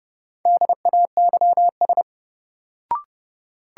Morse code, heard as a keyed tone, about 30 wpm, 700 Hz. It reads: DAYS
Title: Sentences from Top 1000 Words - Review Part 5 - New Word Drill - 30wpm